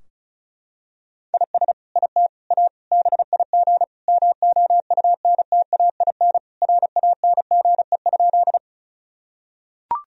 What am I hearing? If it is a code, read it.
Morse code, heard as a keyed tone, about 34 wpm, 700 Hz. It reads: IS IT A BIG MOUNTAIN RANGE?
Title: Sentences from Top 1000 Words - Review Part 4 - 35wpm